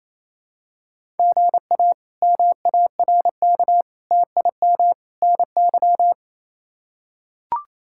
Morse code, heard as a keyed tone, about 27 wpm, 700 Hz. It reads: GA MARK TIM NY
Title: K1USN SST QSO Practice - 28wpm